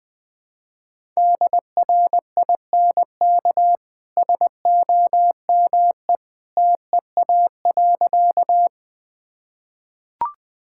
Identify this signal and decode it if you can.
Morse code, heard as a keyed tone, about 20 wpm, 700 Hz. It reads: DRINK SOME TEA.
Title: Easy Sentences with Hints - 20wpm